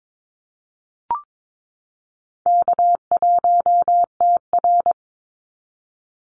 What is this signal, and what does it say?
Morse code, heard as a keyed tone, about 22 wpm, 700 Hz. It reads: K1TR
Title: Easy Call Signs - 22wpm